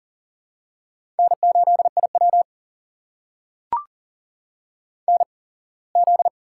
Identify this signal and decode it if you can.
Morse code, heard as a keyed tone, about 38 wpm, spaced as for 24 wpm, 700 Hz. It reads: N8IW NZ
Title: USA Call Signs - 4 Characters - 8x Character Spacing - 40wpm